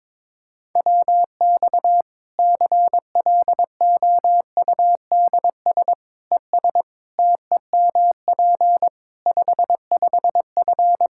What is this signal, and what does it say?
Morse code, heard as a keyed tone, about 22 wpm, 700 Hz. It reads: WX CLOUDS ES TEMP 55F
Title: K1USN SST QSO Practice 22wpm